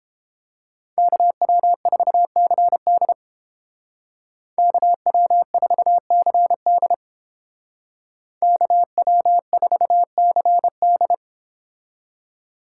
Morse code, heard as a keyed tone, about 30 wpm, 700 Hz. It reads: KW4CD KW4CD KW4CD